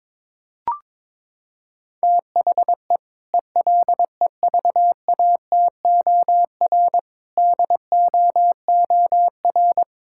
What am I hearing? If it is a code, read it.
Morse code, heard as a keyed tone, about 22 wpm, 700 Hz. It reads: THE ELEVATOR DOOR
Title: Sentences from Top 1600 Words 22wpm